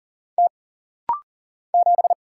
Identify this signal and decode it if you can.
Morse code, heard as a keyed tone, about 40 wpm, 700 Hz. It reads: T 7
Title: Single Letter-Number - Rapid-Fire 40wpm